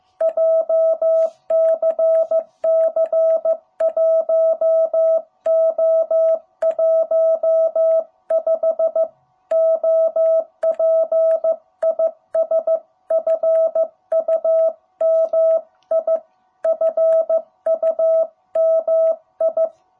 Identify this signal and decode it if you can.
Morse code, heard as a keyed tone, about 15 wpm, 660 Hz. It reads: JCC 1O15 OP IS FUMI FUMI
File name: jh8cbh / 2021cw